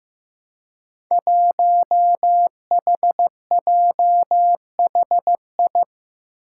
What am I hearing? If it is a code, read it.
Morse code, heard as a keyed tone, about 15 wpm, 700 Hz. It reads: 1HJHI
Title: Five Challenging Letter-Number Combinations 15wpm